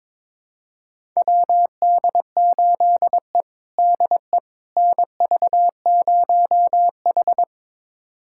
Morse code, heard as a keyed tone, about 22 wpm, 700 Hz. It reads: WD8E DE NV0H
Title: USA Callsign DE Callsign - 4 Characters - 22wpm